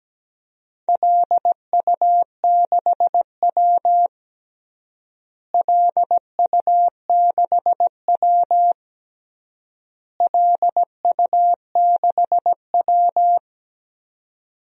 Morse code, heard as a keyed tone, about 17 wpm, 700 Hz. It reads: LU6W LU6W LU6W
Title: DXCC-100 Call Signs - 4 Characters - Repeated 3x - 17wpm